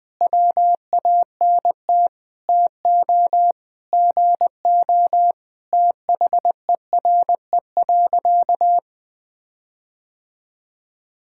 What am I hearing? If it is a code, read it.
Morse code, heard as a keyed tone, about 21 wpm, 700 Hz. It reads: WANT TO GO THERE.
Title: Sentences with 4-Word Phrases - Spoken Beginning - 20wpm